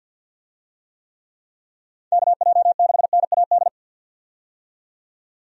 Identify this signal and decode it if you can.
Morse code, heard as a keyed tone, about 52 wpm, 700 Hz. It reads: KJ6NAD
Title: USA Call Signs - 6 Characters - 50wpm